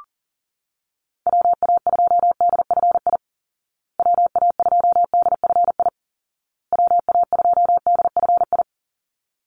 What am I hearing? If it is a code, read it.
Morse code, heard as a keyed tone, about 40 wpm, 700 Hz. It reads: WA2DFI WA2DFI WA2DFI
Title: Call Signs Repeated 2x - 40wpm